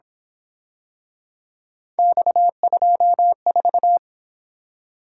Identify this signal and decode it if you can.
Morse code, heard as a keyed tone, about 26 wpm, 700 Hz. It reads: X24
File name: Speed-Racing - Three Letter-Number Combinations - 22wpm